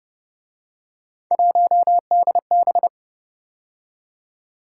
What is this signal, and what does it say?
Morse code, heard as a keyed tone, about 29 wpm, 700 Hz. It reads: 1DB